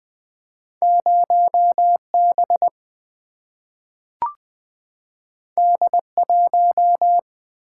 Morse code, heard as a keyed tone, about 20 wpm, 700 Hz. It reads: 0 B D 1
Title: Single-Letter Single-Number 20wpm